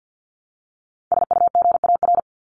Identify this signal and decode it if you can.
Morse code, heard as a keyed tone, about 48 wpm, 700 Hz. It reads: SUGAR
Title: Top 1001-1250 Words 50wpm